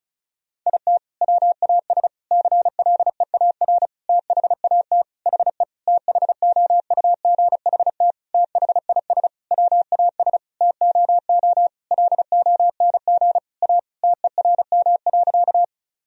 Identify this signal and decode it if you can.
Morse code, heard as a keyed tone, about 34 wpm, 700 Hz. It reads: IT WAS CLEAR THAT HE THOUGHT THIS WAS TOO LONG A TERM.
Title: Sentences from Top 1000 Words - Review Part 4 - 35wpm